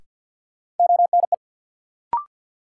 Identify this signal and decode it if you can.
Morse code, heard as a keyed tone, about 51 wpm, 700 Hz. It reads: ONE
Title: Sentences from Top 100 Words - New Word Drill - 50wpm